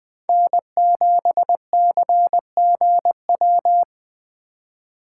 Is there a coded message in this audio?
Yes: Morse code from 0.3 to 3.8 s, heard as a keyed tone, about 20 wpm, 700 Hz. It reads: N7CGW